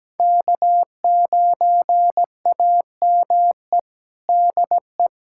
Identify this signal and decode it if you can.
Morse code, heard as a keyed tone, about 17 wpm, 700 Hz. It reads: K9AME DE
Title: USA Callsign DE Callsign - 5 Characters - 17wpm